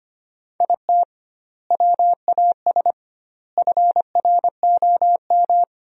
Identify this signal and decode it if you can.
Morse code, heard as a keyed tone, about 25 wpm, 700 Hz. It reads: IT WAS FROM
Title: 2x Word-Spacing - Sentences from Top 100 Words - 25wpm